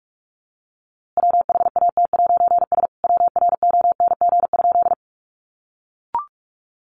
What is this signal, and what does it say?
Morse code, heard as a keyed tone, about 45 wpm, 700 Hz. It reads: WHAT'S WRONG?
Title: A Sound Of Thunder 45wpm